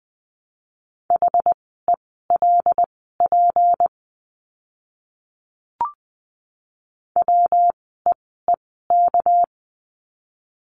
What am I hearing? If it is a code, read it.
Morse code, heard as a keyed tone, about 20 wpm, spaced as for 14 wpm, 700 Hz. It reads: HELP WEEK
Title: 237 Lesson 38f - Letter q - Words 20wpm